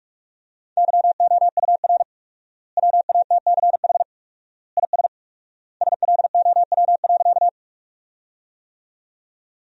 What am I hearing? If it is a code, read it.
Morse code, heard as a keyed tone, about 46 wpm, 700 Hz. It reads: YOUR WATCH IS SLOW.
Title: 4x Word-Spacing - Sentences from Top 400 Words - 45wpm